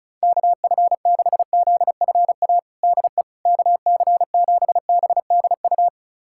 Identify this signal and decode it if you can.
Morse code, heard as a keyed tone, about 36 wpm, 700 Hz. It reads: KF6ZFA DE KC7BDU
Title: USA Callsign DE Callsign - 6 Characters - 35wpm